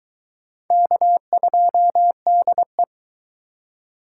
Morse code, heard as a keyed tone, about 22 wpm, 700 Hz. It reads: K2DE